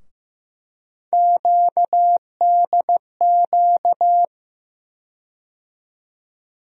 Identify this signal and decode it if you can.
Morse code, heard as a keyed tone, about 15 wpm, 700 Hz. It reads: QDQ
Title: USA Call Sign Postfixes - 3 Letters 15wpm